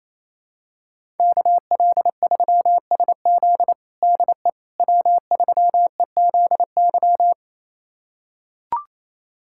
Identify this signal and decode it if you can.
Morse code, heard as a keyed tone, about 28 wpm, 700 Hz. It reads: KL3SZ DE W3EZY
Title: USA Callsign DE Callsign - 5 Characters - 28wpm